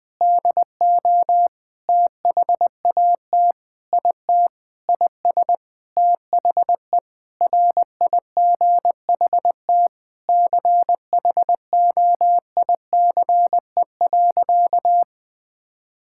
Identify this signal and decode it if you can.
Morse code, heard as a keyed tone, about 20 wpm, 700 Hz. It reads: DO THAT IT IS THE RIGHT CHOICE.
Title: Sentences with 8-Word Phrases - 20wpm